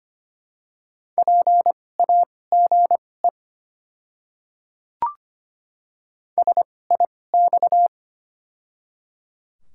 Morse code, heard as a keyed tone, about 25 wpm, spaced as for 18 wpm, 700 Hz. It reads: PAGE SIX
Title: 2x Character Spacing - Top 500 Words - 25wpm